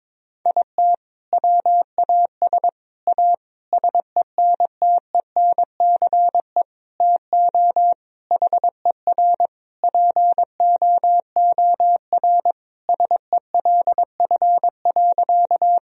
Morse code, heard as a keyed tone, about 21 wpm, 700 Hz. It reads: IT WAS A SENTENCE TO HER POOR SELF.
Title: Sentences from Top 1000 Words - Review Part 1 - 22wpm